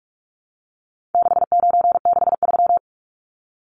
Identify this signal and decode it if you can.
Morse code, heard as a keyed tone, about 44 wpm, 700 Hz. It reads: 6963
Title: Numbers - 4-Digit 45wpm